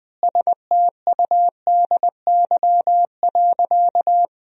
Morse code, heard as a keyed tone, about 20 wpm, 700 Hz. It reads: STUDY.
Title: 4x Word-Spacing - Sentences from Top 300 Words - 20wpm